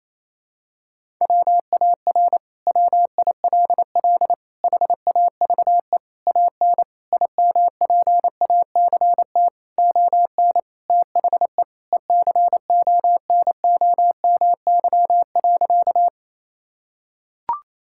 Morse code, heard as a keyed tone, about 28 wpm, 700 Hz. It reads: WAR WILL HAVE AN IMPACT ON THE ECONOMY.
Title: Sentences from Top 1000 Words - Review Part 6 - 28wpm